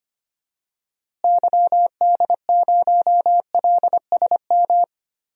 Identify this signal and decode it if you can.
Morse code, heard as a keyed tone, about 25 wpm, 700 Hz. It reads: YD0LSM
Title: DXCC-100 Call Signs - 6 Characters - Repeated 3x - 25wpm